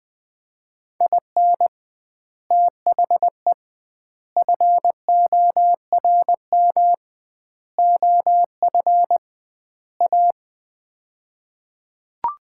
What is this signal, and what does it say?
Morse code, heard as a keyed tone, about 21 wpm, 700 Hz. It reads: IN THE FORM OF A
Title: 2x Word-Spacing - Sets of 5 Words - 20wpm